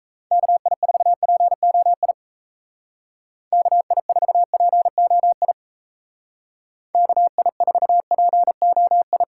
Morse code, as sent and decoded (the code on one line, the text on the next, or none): -.- .. ....- .--. --- .. / -.- .. ....- .--. --- .. / -.- .. ....- .--. --- ..
KI4POI KI4POI KI4POI